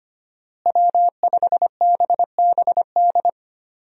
Morse code, heard as a keyed tone, about 25 wpm, 700 Hz. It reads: W5BBD
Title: USA Call Signs - 5 Characters - Repeated 2x - 25wpm